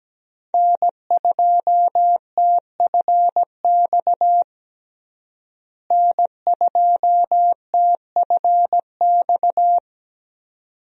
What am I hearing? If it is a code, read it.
Morse code, heard as a keyed tone, about 17 wpm, 700 Hz. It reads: N2TFX N2TFX
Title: USA Call Signs - 5 Characters - Repeated 2x - 17wpm